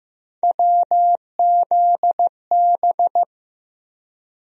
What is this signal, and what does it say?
Morse code, heard as a keyed tone, about 15 wpm, 700 Hz. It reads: WZB
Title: USA Call Sign Postfixes - 3 Letters 15wpm